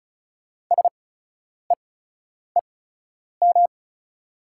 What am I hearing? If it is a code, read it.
Morse code, heard as a keyed tone, about 34 wpm, spaced as for 10 wpm, 690 Hz. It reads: SEEM
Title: 8x Character Spacing - Top 500 Words - 35wpm